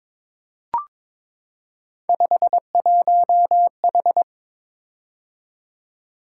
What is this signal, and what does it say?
Morse code, heard as a keyed tone, about 23 wpm, 700 Hz. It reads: 51H
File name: practice / 22wpm / Three Letter-Number Combinations - 22wpm